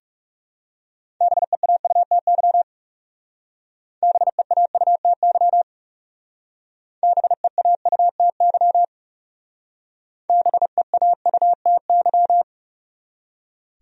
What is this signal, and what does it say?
Morse code, heard as a keyed tone, about 37 wpm, 700 Hz. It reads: BEAUTY BEAUTY BEAUTY BEAUTY
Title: Speed-Racing Top 500 Words 30wpm